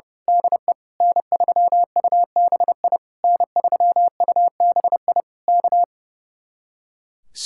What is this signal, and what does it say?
Morse code, heard as a keyed tone, about 30 wpm, 700 Hz. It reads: DE N3UBI N3UBI K
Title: Speed-Racing Common QSO Phrases - 30wpm